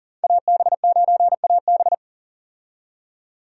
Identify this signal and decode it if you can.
Morse code, heard as a keyed tone, about 37 wpm, 700 Hz. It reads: AB9AB
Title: USA Call Signs - 5 Characters - 40wpm